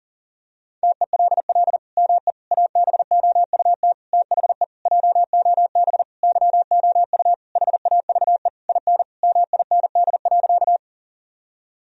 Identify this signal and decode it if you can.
Morse code, heard as a keyed tone, about 40 wpm, 700 Hz. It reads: TELL ME ABOUT THE JOB YOU HAVE IN MIND.
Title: Sentences from Top 1000 Words - Review Part 3 - 40wpm